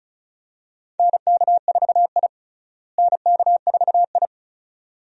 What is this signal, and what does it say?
Morse code, heard as a keyed tone, about 36 wpm, 700 Hz. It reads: NK4I NK4I
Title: USA Call Signs - 4 Characters - Repeated 2x - 35wpm